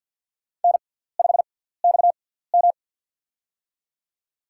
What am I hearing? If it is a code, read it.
Morse code, heard as a keyed tone, about 52 wpm, spaced as for 17 wpm, 700 Hz. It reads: N5XM
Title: USA Call Signs - 6x Character Spacing - 50wpm